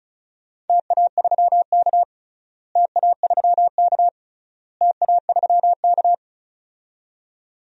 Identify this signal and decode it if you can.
Morse code, heard as a keyed tone, about 36 wpm, 700 Hz. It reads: TA3K TA3K TA3K
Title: DXCC-100 Call Signs - 4 Characters - Repeated 3x - 35wpm